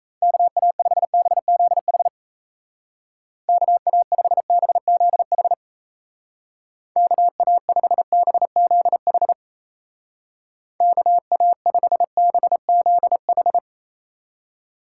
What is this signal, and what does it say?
Morse code, heard as a keyed tone, about 34 wpm, 700 Hz. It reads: KA5BZH KA5BZH KA5BZH KA5BZH
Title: Speed Racing - USA 2x3 Call Signs 28wpm